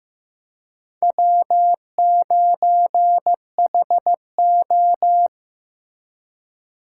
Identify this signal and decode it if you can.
Morse code, heard as a keyed tone, about 15 wpm, 700 Hz. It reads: W9HO
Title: USA Call Signs - 4 Characters - 6x Character Spacing - 15wpm